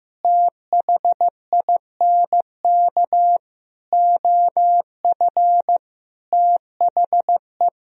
Morse code, heard as a keyed tone, about 15 wpm, 700 Hz. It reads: THINK OF THE
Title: Sentences from Top 300 Words and Abbreviations 15wpm